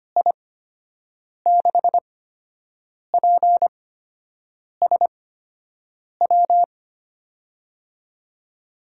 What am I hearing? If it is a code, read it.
Morse code, heard as a keyed tone, about 25 wpm, spaced as for 7 wpm, 700 Hz. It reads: I6PSW
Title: USA Call Signs - 8x Character Spacing - 25wpm